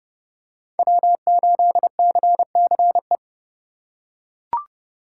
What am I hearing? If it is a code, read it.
Morse code, heard as a keyed tone, about 29 wpm, 700 Hz. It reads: W8CCE